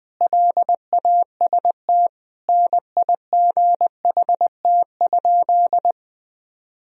Morse code, heard as a keyed tone, about 20 wpm, 700 Hz. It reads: LAST NIGHT?
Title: Sentences from Top 1000 Words - Review Part 4 - 20wpm